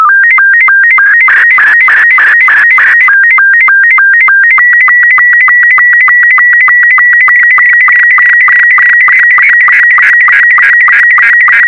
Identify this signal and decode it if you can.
SSTV (slow-scan television), Robot 72.
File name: sstv / R72-VK4MSL